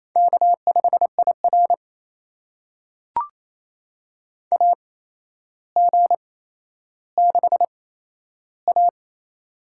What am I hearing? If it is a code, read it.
Morse code, heard as a keyed tone, about 28 wpm, spaced as for 12 wpm, 700 Hz. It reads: K5IR AG6A
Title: USA Call Signs - 8x Character Spacing - 28wpm